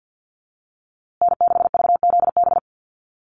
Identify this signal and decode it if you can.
Morse code, heard as a keyed tone, about 48 wpm, 700 Hz. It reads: N6VZB
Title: USA Call Signs Repeated 3x 50wpm